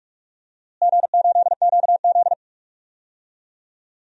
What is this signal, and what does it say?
Morse code, heard as a keyed tone, about 46 wpm, 700 Hz. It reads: G8QZ
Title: DXCC-100 Call Signs - 4 Characters - Repeated 2x - 45wpm